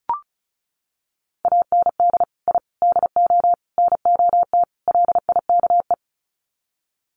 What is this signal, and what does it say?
Morse code, heard as a keyed tone, about 34 wpm, 700 Hz. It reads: AND I DO NOT LIKE
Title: Sets of 5 Words Encore2 35wpm